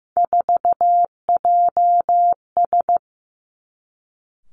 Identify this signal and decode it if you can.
Morse code, heard as a keyed tone, about 15 wpm, 700 Hz. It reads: 4JS
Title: Three Challenging Letter-Number Combinations 15wpm